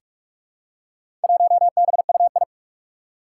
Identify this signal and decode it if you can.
Morse code, heard as a keyed tone, about 44 wpm, 700 Hz. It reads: 1BUI
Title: Four Challenging Letter-Number Combinations 45wpm